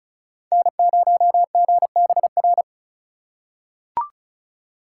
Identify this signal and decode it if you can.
Morse code, heard as a keyed tone, about 36 wpm, 700 Hz. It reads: N0GBR
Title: USA Call Signs - 2x Character Spacing - 35wpm